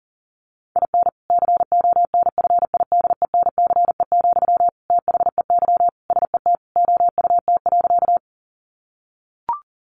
Morse code, heard as a keyed tone, about 40 wpm, 700 Hz. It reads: IN CONFIDENCE, THEY SET OUT.
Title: Sentences from Top 1500 Words 40wpm